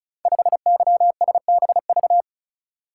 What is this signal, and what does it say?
Morse code, heard as a keyed tone, about 36 wpm, 700 Hz. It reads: 5YSBV
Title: Code Groups - 5 Letter:Numbers - 35wpm